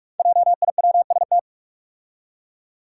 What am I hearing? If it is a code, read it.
Morse code, heard as a keyed tone, about 44 wpm, 700 Hz. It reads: JIWST
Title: Five Challenging Letter-Number Combinations 45wpm